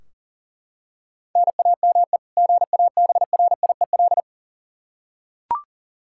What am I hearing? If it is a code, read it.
Morse code, heard as a keyed tone, about 40 wpm, 700 Hz. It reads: NAME GABRIEL
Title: K1USN SST QSO Practice 40wpm